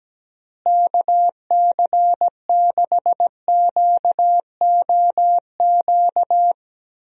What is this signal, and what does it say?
Morse code, heard as a keyed tone, about 17 wpm, 700 Hz. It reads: KC6QOQ
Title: USA Call Signs - 6 Characters - Repeated 3x - 17wpm